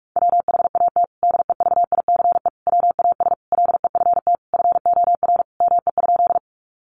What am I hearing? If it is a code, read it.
Morse code, heard as a keyed tone, about 44 wpm, 700 Hz. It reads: WHAT DEVICE WAS LEFT FOR ME?
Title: Sentences from Top 1500 Words 45wpm